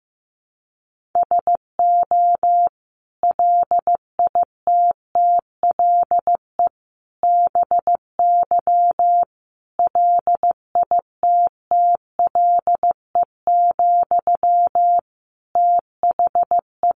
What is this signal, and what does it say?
Morse code, heard as a keyed tone, about 14 wpm, 700 Hz. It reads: SO LITTLE BY LITTLE, THE
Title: Sentences from Top 1500 Words 15wpm